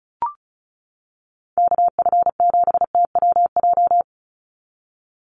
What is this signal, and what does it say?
Morse code, heard as a keyed tone, about 34 wpm, 700 Hz. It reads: KF7TWJ
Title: USA Call Signs 35wpm